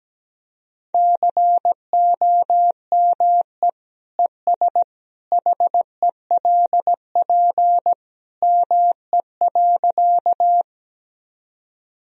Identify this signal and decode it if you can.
Morse code, heard as a keyed tone, about 17 wpm, 700 Hz. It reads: COME ES HELP ME.
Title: Sentences from Top 200 Words and Abbreviations 17wpm